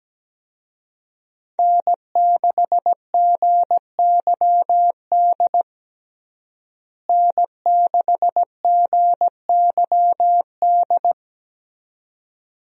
Morse code, heard as a keyed tone, about 17 wpm, 700 Hz. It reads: N6GYD N6GYD
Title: USA Call Signs - 5 Characters - Repeated 2x - 17wpm